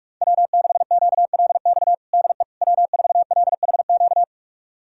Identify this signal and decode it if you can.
Morse code, heard as a keyed tone, about 46 wpm, 700 Hz. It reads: W6QLX DE W4LHQ
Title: USA Callsign DE Callsign - 5 Characters - 45wpm